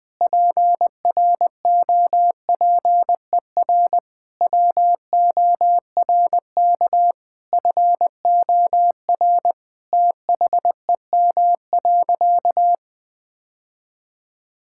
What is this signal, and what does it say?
Morse code, heard as a keyed tone, about 21 wpm, 700 Hz. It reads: PROPER WORK FOR THEM.
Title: Sentences from Top 1000 Words - Review Part 5 - 20wpm